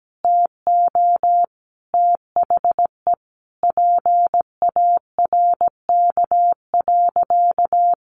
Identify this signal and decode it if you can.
Morse code, heard as a keyed tone, about 17 wpm, 700 Hz. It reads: TO THE PARK.
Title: Sentences from Top 1300 Words 17wpm